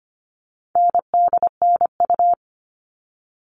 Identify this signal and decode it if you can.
Morse code, heard as a keyed tone, about 25 wpm, 700 Hz. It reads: NDNU
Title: All Cut Numbers - 4-Digit 25wpm